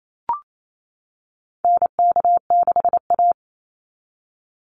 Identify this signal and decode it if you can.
Morse code, heard as a keyed tone, about 28 wpm, 700 Hz. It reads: NK6A
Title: NA SOTA Chaser Callsigns 28wpm